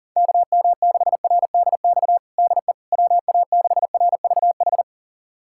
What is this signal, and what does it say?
Morse code, heard as a keyed tone, about 40 wpm, 700 Hz. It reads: KM6RDX DE WA6RVH
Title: USA Callsign DE Callsign - 6 Characters - 40wpm